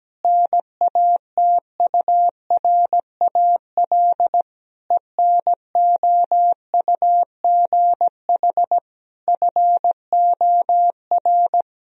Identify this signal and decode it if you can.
Morse code, heard as a keyed tone, about 17 wpm, 700 Hz. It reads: NATURAL ENOUGH FOR
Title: Sentences from Top 1000 Words - Review Part 4 - 17wpm